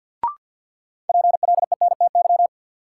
Morse code, heard as a keyed tone, about 53 wpm, 700 Hz. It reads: PLENTY
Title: Sentences from Top 1300 Words - New Word Drill Alt - 50wpm